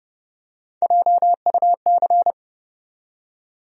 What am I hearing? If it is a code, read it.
Morse code, heard as a keyed tone, about 30 wpm, 700 Hz. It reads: JUC
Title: USA Call Sign Postfixes - 3 Letters 30wpm